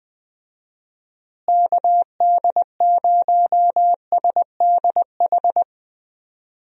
Morse code, heard as a keyed tone, about 21 wpm, 700 Hz. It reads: KD0SDH
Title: USA Call Signs - 6 Characters - Repeated 3x - 20wpm